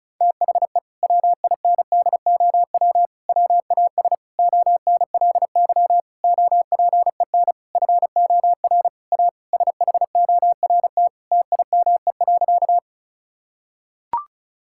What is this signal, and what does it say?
Morse code, heard as a keyed tone, about 36 wpm, 700 Hz. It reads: THE WINDOW WAS ONLY OPEN FOR A SHORT TIME.
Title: Sentences from Top 1600 Words 35wpm